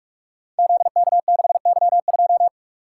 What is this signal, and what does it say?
Morse code, heard as a keyed tone, about 43 wpm, 700 Hz. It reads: ZK6Y2
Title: Code Groups - 5 Letter:Numbers - 45wpm